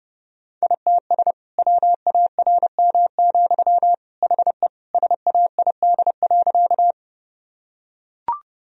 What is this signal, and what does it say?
Morse code, heard as a keyed tone, about 30 wpm, 700 Hz. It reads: ITS WARM, HE SAID.
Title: Sentences from Top 1000 Words - Review Part 2 - 30wpm